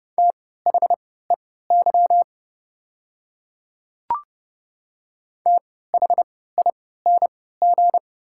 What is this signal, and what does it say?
Morse code, heard as a keyed tone, about 29 wpm, spaced as for 17 wpm, 690 Hz. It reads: THEY THING
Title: 3x Character Spacing - Top 100 Words - 30wpm